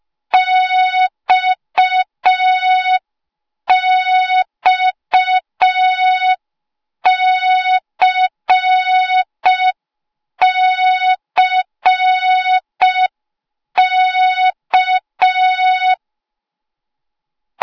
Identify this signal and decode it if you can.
Morse code, heard as a keyed tone, about 5 wpm, 3700 Hz. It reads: XXCCK